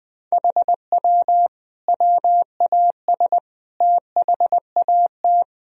Morse code, heard as a keyed tone, about 20 wpm, 700 Hz. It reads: HW WAS THAT
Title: Sentences from Top 500 Words and Abbreviations 20wpm